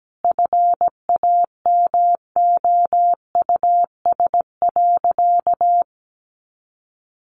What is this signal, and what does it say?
Morse code, heard as a keyed tone, about 17 wpm, 700 Hz. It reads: FAMOUS.
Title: Sentences from Top 1500 Words 17wpm